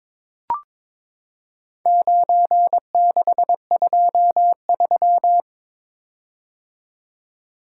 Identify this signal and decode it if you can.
Morse code, heard as a keyed tone, about 23 wpm, 700 Hz. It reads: 9623
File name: practice / 22wpm / Four Letter-Number Combinations - 22wpm